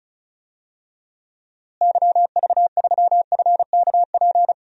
Morse code, heard as a keyed tone, about 34 wpm, 700 Hz. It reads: YV3FKP
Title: DXCC-100 Call Signs - 6 Characters - Repeated 2x - 35wpm